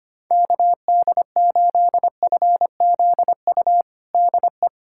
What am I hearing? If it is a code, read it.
Morse code, heard as a keyed tone, about 25 wpm, 700 Hz. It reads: KD8FZU DE
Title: USA Callsign DE Callsign 25wpm